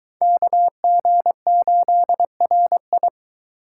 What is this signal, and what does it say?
Morse code, heard as a keyed tone, about 23 wpm, 700 Hz. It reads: KG8RI